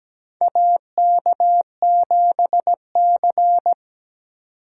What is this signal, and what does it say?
Morse code, heard as a keyed tone, about 17 wpm, 700 Hz. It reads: AK7C
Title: USA Call Signs - 6x Character Spacing - 17wpm